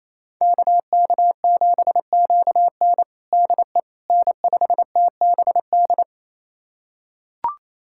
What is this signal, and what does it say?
Morse code, heard as a keyed tone, about 28 wpm, 700 Hz. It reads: KK7QN DE N5TBD
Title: USA Callsign DE Callsign - 5 Characters - 28wpm